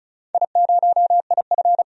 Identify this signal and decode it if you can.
Morse code, heard as a keyed tone, about 36 wpm, 700 Hz. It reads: I0IF
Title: DXCC-100 Call Signs - 4 Characters - 35wpm